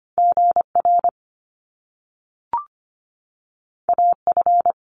Morse code, heard as a keyed tone, about 25 wpm, 700 Hz. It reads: GR AF
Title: Top Two Letter Combinations in Words 25wpm